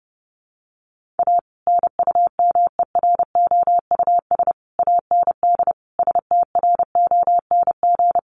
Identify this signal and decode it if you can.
Morse code, heard as a keyed tone, about 30 wpm, 700 Hz. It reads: A NUMEROUS AND STRONG
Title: Sentences from Top 1400 Words 30wpm